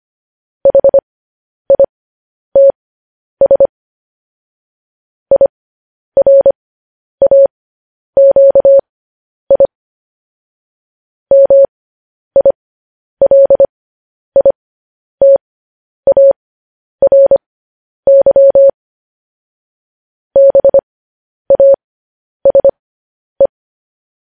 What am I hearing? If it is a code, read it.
Morse code, heard as a keyed tone, about 25 wpm, spaced as for 10 wpm, 550 Hz. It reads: HITS IRAQI MILITARY BASE